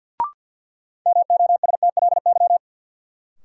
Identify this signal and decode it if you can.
Morse code, heard as a keyed tone, about 53 wpm, 700 Hz. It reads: MOSTLY
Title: Sentences from Top 1300 Words - New Word Drill Alt - 50wpm